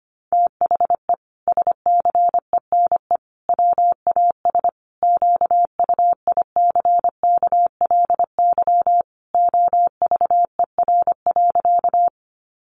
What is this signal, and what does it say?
Morse code, heard as a keyed tone, about 25 wpm, 700 Hz. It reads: THE SCENE WAS QUICKLY OVER.
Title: Sentences from Top 1100 Words 25wpm